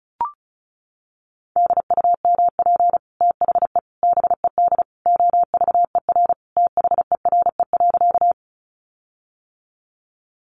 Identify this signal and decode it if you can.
Morse code, heard as a keyed tone, about 34 wpm, 700 Hz. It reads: DUMP THE BED OVER THERE.
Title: Sentences from Top 1300 Words 35wpm